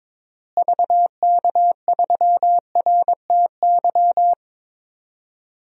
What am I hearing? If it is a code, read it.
Morse code, heard as a keyed tone, about 22 wpm, 700 Hz. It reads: VK3RTY
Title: DXCC-100 Call Signs - 6 Characters - Repeated 3x - 22wpm